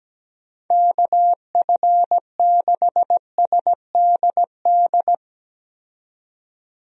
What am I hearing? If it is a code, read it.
Morse code, heard as a keyed tone, about 17 wpm, 700 Hz. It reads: KF6SDD
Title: USA Call Signs - 6 Characters - 17wpm